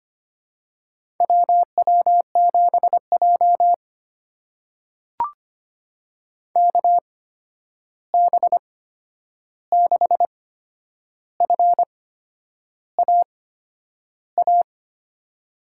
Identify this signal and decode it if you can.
Morse code, heard as a keyed tone, about 25 wpm, spaced as for 9 wpm, 700 Hz. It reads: WW7J KB6FAA
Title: USA Call Signs - 8x Character Spacing - 25wpm